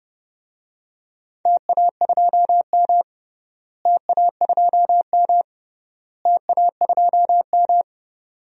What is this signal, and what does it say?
Morse code, heard as a keyed tone, about 31 wpm, 700 Hz. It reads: TA2M TA2M TA2M
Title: DXCC-100 Call Signs - 4 Characters - Repeated 3x - 30wpm